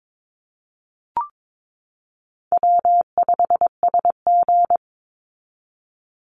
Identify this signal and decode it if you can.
Morse code, heard as a keyed tone, about 22 wpm, 700 Hz. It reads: W5SG
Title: Easy Call Signs Repeated 2x - 22wpm